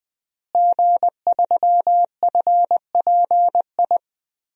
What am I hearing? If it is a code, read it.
Morse code, heard as a keyed tone, about 21 wpm, 700 Hz. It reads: G3FPI
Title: DXCC-100 Call Signs - 5 Characters - Repeated 2x - 20wpm